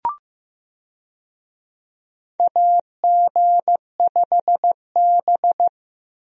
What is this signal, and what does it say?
Morse code, heard as a keyed tone, about 16 wpm, 700 Hz. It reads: AG5B
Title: USA Call Signs - 4 Characters - Repeated 2x - 15wpm